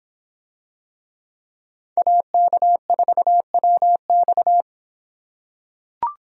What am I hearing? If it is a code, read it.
Morse code, heard as a keyed tone, about 24 wpm, 700 Hz. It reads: AK4WX